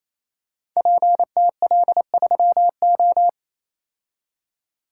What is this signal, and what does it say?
Morse code, heard as a keyed tone, about 28 wpm, 700 Hz. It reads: PTL3O
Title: Code Groups - 5 Letter:Numbers - 28wpm